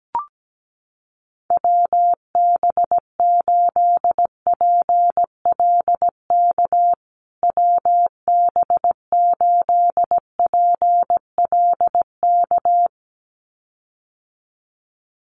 Morse code, heard as a keyed tone, about 17 wpm, 700 Hz. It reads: WB8PLK WB8PLK
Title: USA Call Signs Repeated 2x 17wpm